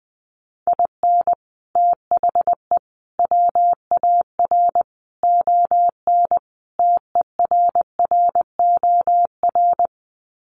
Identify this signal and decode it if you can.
Morse code, heard as a keyed tone, about 20 wpm, 700 Hz. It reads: IN THE WAR ON TERROR
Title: Sets of 5 Words Encore2 20wpm